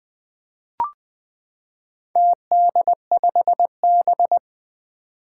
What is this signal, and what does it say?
Morse code, heard as a keyed tone, about 21 wpm, 700 Hz. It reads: TD5B
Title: Four Challenging Letter-Number Combinations 20wpm